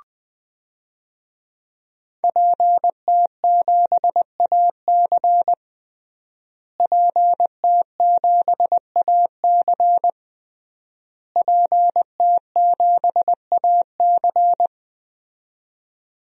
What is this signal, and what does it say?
Morse code, heard as a keyed tone, about 21 wpm, 700 Hz. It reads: PT7AC PT7AC PT7AC
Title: DXCC-100 Call Signs - 5 Characters - Repeated 3x - 20wpm